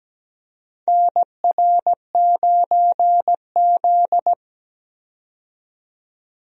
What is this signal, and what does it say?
Morse code, heard as a keyed tone, about 17 wpm, 700 Hz. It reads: NR9Z
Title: USA Call Signs - 4 Characters - 17wpm